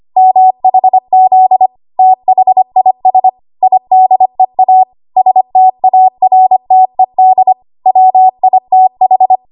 Morse code, heard as a keyed tone, about 25 wpm, 750 Hz. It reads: MHZ THIS IDEA STARTED WITH
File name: Morse / 25WPM